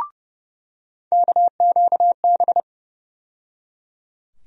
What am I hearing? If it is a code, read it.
Morse code, heard as a keyed tone, about 31 wpm, 700 Hz. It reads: KQB
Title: USA Call Sign Postfixes - 3 Letters 30wpm